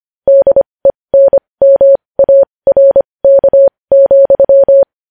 Morse code, heard as a keyed tone, about 25 wpm, spaced as for 20 wpm, 550 Hz. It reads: DENMARK,
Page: news-headlines-in-morse-code-at-20-wpm